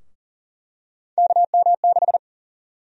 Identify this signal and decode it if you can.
Morse code, heard as a keyed tone, about 40 wpm, 700 Hz. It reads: KM6
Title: USA Call Sign Prefixes - Repeated 2x - 40wpm